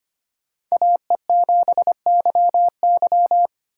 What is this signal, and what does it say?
Morse code, heard as a keyed tone, about 25 wpm, 700 Hz. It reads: AE7YY
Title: USA Call Signs - 4x Character Spacing - 25wpm